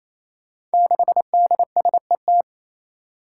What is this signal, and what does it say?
Morse code, heard as a keyed tone, about 28 wpm, 700 Hz. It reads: 6DSET
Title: Five Challenging Letter-Number Combinations 28wpm